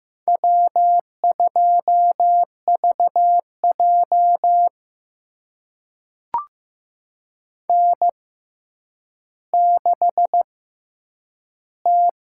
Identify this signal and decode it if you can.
Morse code, heard as a keyed tone, about 15 wpm, spaced as for 9 wpm, 700 Hz. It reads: W2VJ N6T
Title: USA Call Signs - 4 Characters - 6x Character Spacing - 15wpm